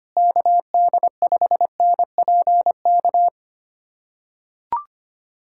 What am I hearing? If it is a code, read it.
Morse code, heard as a keyed tone, about 25 wpm, 700 Hz. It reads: KD5NPK
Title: USA Callsign DE Callsign - 6 Characters - 25wpm